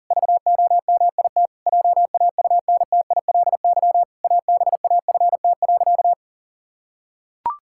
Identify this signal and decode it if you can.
Morse code, heard as a keyed tone, about 40 wpm, 700 Hz. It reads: VOMIT JAUNTILY ABAFT.
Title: Pangram Challenge v2 - 40wpm